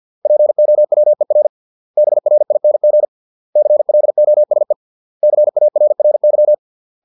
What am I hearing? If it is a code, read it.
Morse code, heard as a keyed tone, about 50 wpm, 600 Hz. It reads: POWER BRING CLOSE CARRY